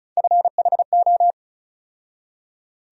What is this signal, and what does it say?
Morse code, heard as a keyed tone, about 34 wpm, 700 Hz. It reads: FHO
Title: USA Call Sign Postfixes - 3 Letters 35wpm